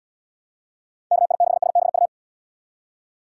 Morse code, heard as a keyed tone, about 72 wpm, 700 Hz. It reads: KI6SBU